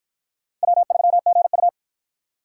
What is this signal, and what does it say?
Morse code, heard as a keyed tone, about 47 wpm, 700 Hz. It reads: W3GU